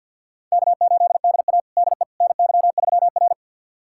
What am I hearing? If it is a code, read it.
Morse code, heard as a keyed tone, about 52 wpm, 700 Hz. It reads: K8DA DE NY3R
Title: USA Callsign DE Callsign - 4 Characters - 50wpm